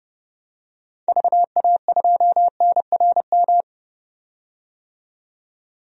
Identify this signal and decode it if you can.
Morse code, heard as a keyed tone, about 32 wpm, 700 Hz. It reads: VA2NRM
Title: DXCC-100 Call Signs - 6 Characters - 30wpm